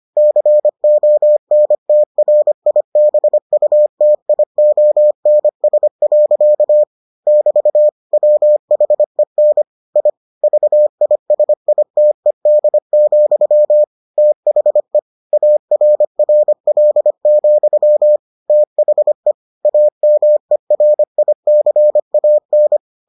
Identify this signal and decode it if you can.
Morse code, heard as a keyed tone, about 25 wpm, 600 Hz. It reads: CONTRIBUTIONS. <BT> WHEN I VISITED, THE ARRL, THE AMERICAN